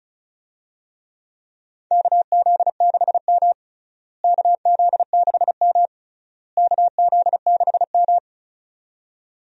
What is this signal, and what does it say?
Morse code, heard as a keyed tone, about 34 wpm, 700 Hz. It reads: KZ6M KZ6M KZ6M
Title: USA Call Signs - 4 Characters - Repeated 3x - 35wpm